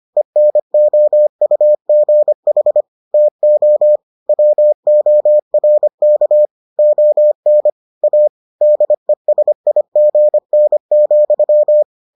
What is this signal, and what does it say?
Morse code, heard as a keyed tone, about 25 wpm, 600 Hz. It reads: ENOUGH TO WORK ON A DESIGN,